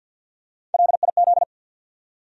Morse code, heard as a keyed tone, about 53 wpm, 700 Hz. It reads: LIZ
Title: USA Common Names - 50wpm